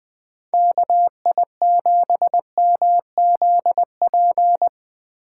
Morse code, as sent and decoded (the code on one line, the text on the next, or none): -.- .. --... -- --.. .--.
KI7MZP